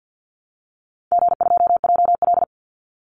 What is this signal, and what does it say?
Morse code, heard as a keyed tone, about 48 wpm, 700 Hz. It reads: G2JL
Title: Easy Call Signs Repeated 2x - 50wpm